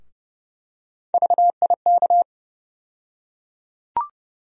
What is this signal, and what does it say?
Morse code, heard as a keyed tone, about 31 wpm, 700 Hz. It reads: VIK
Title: USA Call Sign Postfixes - 3 Letters 30wpm